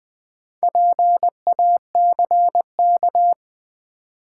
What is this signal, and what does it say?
Morse code, heard as a keyed tone, about 20 wpm, 700 Hz. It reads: PACK
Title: Sentences from Top 1300 Words - New Word Drill - 20wpm